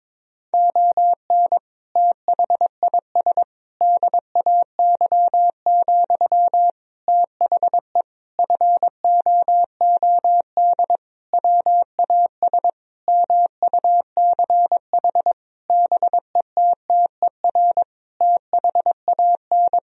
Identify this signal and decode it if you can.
Morse code, heard as a keyed tone, about 22 wpm, 700 Hz. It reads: ON THIS DAY, THE FOOD WAS MUCH BETTER THAN
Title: Sentences from Top 1000 Words - Review Part 2 - 22wpm